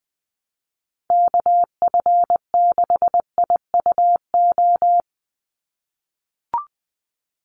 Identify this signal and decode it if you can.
Morse code, heard as a keyed tone, about 20 wpm, 700 Hz. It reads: KF6IUO
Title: USA Call Signs 20wpm